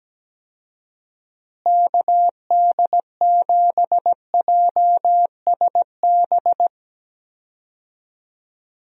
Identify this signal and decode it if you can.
Morse code, heard as a keyed tone, about 17 wpm, 700 Hz. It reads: KD7JSB
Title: USA Call Signs - 6 Characters - Repeated 3x - 17wpm